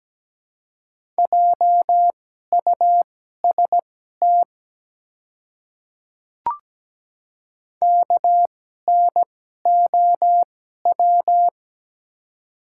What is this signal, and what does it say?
Morse code, heard as a keyed tone, about 17 wpm, spaced as for 12 wpm, 700 Hz. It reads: JUST KNOW
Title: 2x Character Spacing - Top 200 Words - 17wpm